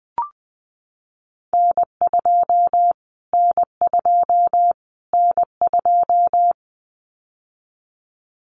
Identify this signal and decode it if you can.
Morse code, heard as a keyed tone, about 20 wpm, 700 Hz. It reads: N2 N2 N2
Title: Callsign Prefixes for Top 25 DXCC - Repeated 3x - 20wpm